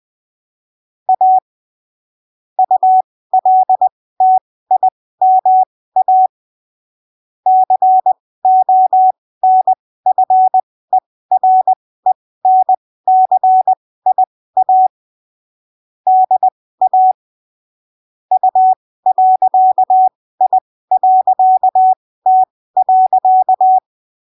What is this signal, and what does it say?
Morse code, heard as a keyed tone, about 20 wpm, spaced as for 15 wpm, 750 Hz. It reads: A ULTIMA CONFERENCIA DA U.I.T.
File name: arquivos / texto / texto 07 20ppm